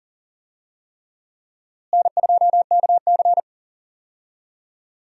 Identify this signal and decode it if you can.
Morse code, heard as a keyed tone, about 40 wpm, 700 Hz. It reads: N2KC
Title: USA Call Signs - 4 Characters - Repeated 3x - 40wpm